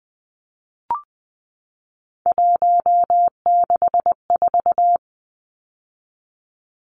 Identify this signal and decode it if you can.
Morse code, heard as a keyed tone, about 20 wpm, 700 Hz. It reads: 164
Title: Numbers - 3-Digit 20wpm